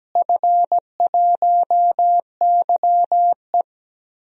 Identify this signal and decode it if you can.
Morse code, heard as a keyed tone, about 17 wpm, 700 Hz. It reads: F1YE
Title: DXCC-100 Call Signs - 4 Characters - Repeated 3x - 17wpm